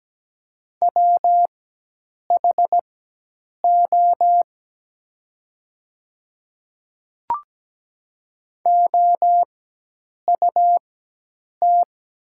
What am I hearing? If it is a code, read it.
Morse code, heard as a keyed tone, about 17 wpm, spaced as for 8 wpm, 700 Hz. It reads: WHO OUT
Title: 4x Character Spacing - Top 100 Words - 17wpm